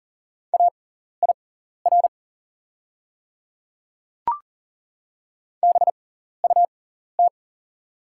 Morse code, heard as a keyed tone, about 40 wpm, spaced as for 14 wpm, 690 Hz. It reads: AIR BUT
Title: 6x Character Spacing - Top 300 Words - 40wpm